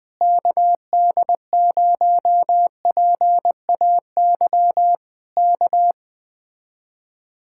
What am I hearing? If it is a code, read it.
Morse code, heard as a keyed tone, about 20 wpm, 700 Hz. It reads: KD0PAY K
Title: Speed-Racing Common QSO Phrases - 17wpm